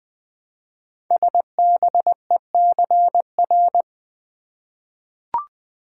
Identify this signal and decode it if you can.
Morse code, heard as a keyed tone, about 20 wpm, 700 Hz. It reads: SBECR
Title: Code Groups - 5 Letters - 20wpm